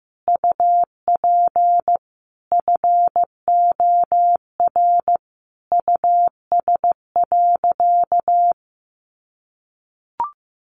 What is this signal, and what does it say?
Morse code, heard as a keyed tone, about 15 wpm, 700 Hz. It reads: UP FOR US.
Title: Sentences from Top 1400 Words 15wpm